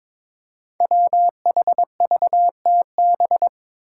Morse code, heard as a keyed tone, about 22 wpm, 700 Hz. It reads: WHVTB
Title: Five Challenging Letter-Number Combinations 22wpm